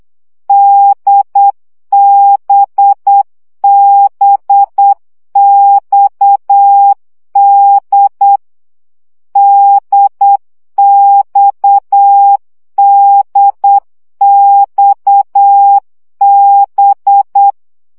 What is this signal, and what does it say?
Morse code, heard as a keyed tone, about 8 wpm, 800 Hz. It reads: DBBXD DXDXB
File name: CW gyakorlas d b x 7 baud